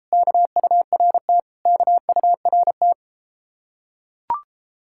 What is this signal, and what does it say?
Morse code, heard as a keyed tone, about 33 wpm, 700 Hz. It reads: KURT KURT